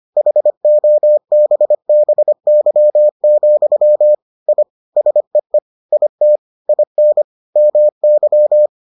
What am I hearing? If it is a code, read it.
Morse code, heard as a keyed tone, about 25 wpm, 600 Hz. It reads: HOBBY, I SEE IT IN MY